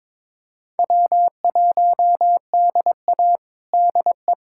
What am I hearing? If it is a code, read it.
Morse code, heard as a keyed tone, about 22 wpm, 700 Hz. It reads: W1DA DE
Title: USA Callsign DE Callsign - 4 Characters - 22wpm